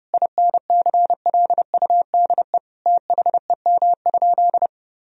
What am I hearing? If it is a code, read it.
Morse code, heard as a keyed tone, about 28 wpm, 700 Hz. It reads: INCLUDE THEM?
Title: Sentences from Top 1000 Words - Review Part 3 - 30wpm